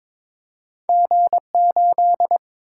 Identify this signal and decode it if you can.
Morse code, heard as a keyed tone, about 22 wpm, 700 Hz. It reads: G8
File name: Two Letter-Number Combinations - 22wpm